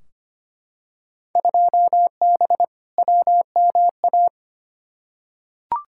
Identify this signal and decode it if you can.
Morse code, heard as a keyed tone, about 25 wpm, 700 Hz. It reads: 2B WMA
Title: Field Day Exchanges Encore - 25wpm